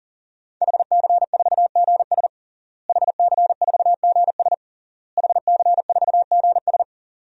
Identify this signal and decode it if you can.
Morse code, heard as a keyed tone, about 40 wpm, 700 Hz. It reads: HC4GS HC4GS HC4GS
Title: DXCC-100 Call Signs - 5 Characters - Repeated 3x - 40wpm